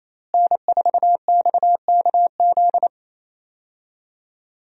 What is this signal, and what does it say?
Morse code, heard as a keyed tone, about 28 wpm, 700 Hz. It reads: N4XKZ
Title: USA Call Signs - 5 Characters - Repeated 2x - 28wpm